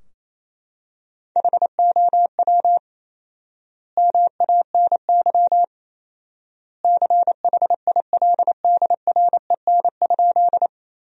Morse code, heard as a keyed tone, about 28 wpm, 700 Hz. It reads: HOW MANY CHILDREN?
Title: 4x Word-Spacing - Sentences from Top 300 Words - 28wpm